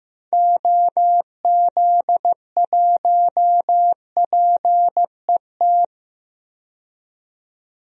Morse code, heard as a keyed tone, about 15 wpm, 700 Hz. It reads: OZ1PET
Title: DXCC-100 Call Signs - 6 Characters - 15wpm